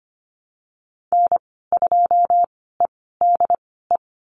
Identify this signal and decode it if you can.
Morse code, heard as a keyed tone, about 25 wpm, spaced as for 16 wpm, 700 Hz. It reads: N2EDE
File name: Lesson 15g - Number 5 - Call Signs 25wpm